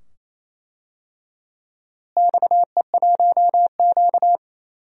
Morse code, heard as a keyed tone, about 28 wpm, 700 Hz. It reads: XE1Q
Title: DXCC-100 Call Signs - 4 Characters - Repeated 2x - 28wpm